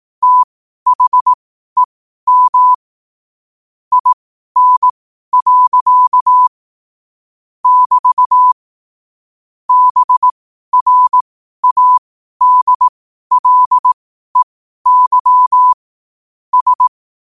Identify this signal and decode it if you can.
Morse code, heard as a keyed tone, about 18 wpm, spaced as for 12 wpm, 1000 Hz. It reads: THEM IN. <BT> BRADLEY S
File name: qotdfiles / 12wpm